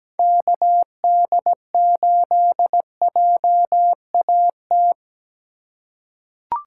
Morse code, heard as a keyed tone, about 17 wpm, 700 Hz. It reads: KD8JAT
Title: USA Call Signs - 2x Character Spacing - 17wpm